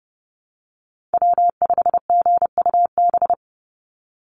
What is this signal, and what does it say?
Morse code, heard as a keyed tone, about 30 wpm, 700 Hz. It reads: W5GUB
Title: USA Call Signs 30wpm